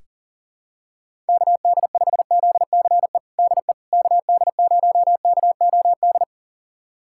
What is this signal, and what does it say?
Morse code, heard as a keyed tone, about 40 wpm, 700 Hz. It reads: KD5ZCE DE KD0KOD
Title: USA Callsign DE Callsign - 6 Characters - 40wpm